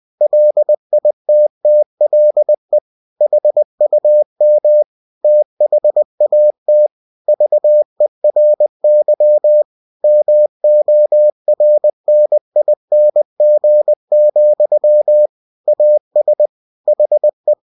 Morse code, heard as a keyed tone, about 20 wpm, 600 Hz. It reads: LITTLE HUM THAT VERY MORNING, AS HE